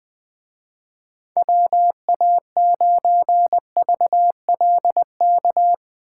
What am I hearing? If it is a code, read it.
Morse code, heard as a keyed tone, about 22 wpm, 700 Hz. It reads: WA9VLK
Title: Speed Racing - USA 2x3 Call Signs 20wpm